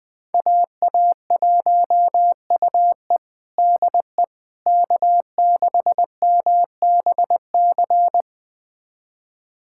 Morse code, heard as a keyed tone, about 20 wpm, 700 Hz. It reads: AA1UE DE K6MBC
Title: USA Callsign DE Callsign - 5 Characters - 20wpm